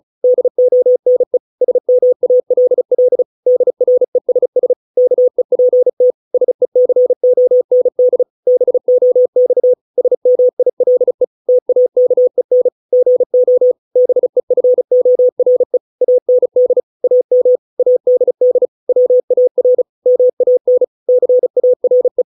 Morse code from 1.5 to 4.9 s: SMALL DRESS